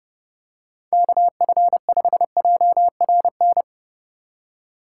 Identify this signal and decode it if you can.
Morse code, heard as a keyed tone, about 28 wpm, 700 Hz. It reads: KF5JRN